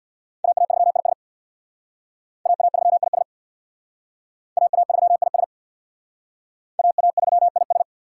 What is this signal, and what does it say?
Morse code, heard as a keyed tone, about 64 wpm, 700 Hz. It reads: AA3IS AA3IS AA3IS AA3IS